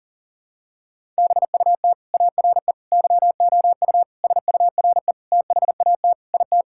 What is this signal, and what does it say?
Morse code, heard as a keyed tone, about 37 wpm, 700 Hz. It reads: BUT ARE YOU SURE THAT IT
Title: Sentences from Top 1600 Words 40wpm